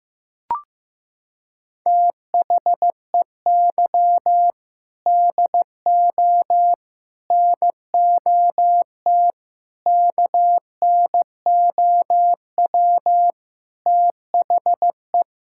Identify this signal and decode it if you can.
Morse code, heard as a keyed tone, about 15 wpm, 700 Hz. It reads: THEY DO NOT KNOW THE
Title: Sentences from Top 1000 Words - Review Part 6 - 15wpm